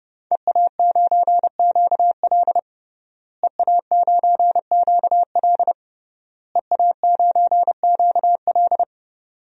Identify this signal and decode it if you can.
Morse code, heard as a keyed tone, about 30 wpm, 700 Hz. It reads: EA9QL EA9QL EA9QL
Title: DXCC-100 Call Signs - 5 Characters - Repeated 3x - 30wpm